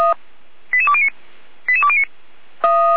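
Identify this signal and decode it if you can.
Five-tone selective call (ZVEI), heard as quick runs of five tones, tones of about 70 ms. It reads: 88209 88209
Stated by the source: Five-tone ZVEI selective call received on an SDR.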